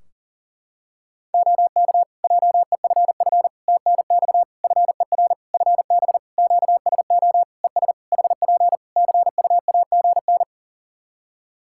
Morse code, heard as a keyed tone, about 40 wpm, 700 Hz. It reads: OK JEFF TNX FER FB QSO ES HP CUAGN
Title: Practice QSO - Exchange 4 - Ending - 40wpm